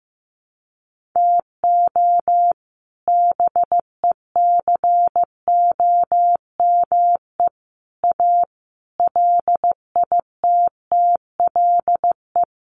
Morse code, heard as a keyed tone, about 15 wpm, 700 Hz. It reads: TO BECOME A LITTLE
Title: Sentences from Top 1300 Words 15wpm